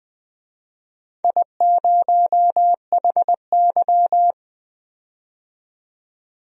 Morse code, heard as a keyed tone, about 20 wpm, 700 Hz. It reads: I0HY
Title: DXCC-100 Call Signs - 4 Characters - 20wpm